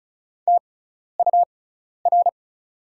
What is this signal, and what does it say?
Morse code, heard as a keyed tone, about 36 wpm, spaced as for 12 wpm, 700 Hz. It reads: TUR
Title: 6x Character Spacing - Top 200 Words - 35wpm